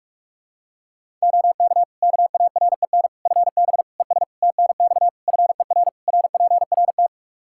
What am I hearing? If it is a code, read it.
Morse code, heard as a keyed tone, about 46 wpm, 700 Hz. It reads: OK KAREN FB ES TNX FER RPRT
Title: Practice QSO - Exchange 2 - Info - 45wpm